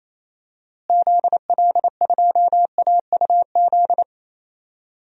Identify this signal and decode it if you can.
Morse code, heard as a keyed tone, about 28 wpm, 700 Hz. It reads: ZL2AUZ
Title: DXCC-100 Call Signs - 6 Characters - Repeated 3x - 28wpm